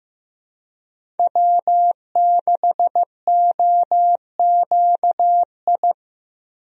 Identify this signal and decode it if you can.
Morse code, heard as a keyed tone, about 15 wpm, 700 Hz. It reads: W6OQI
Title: USA Call Signs - 5 Characters - Repeated 3x - 15wpm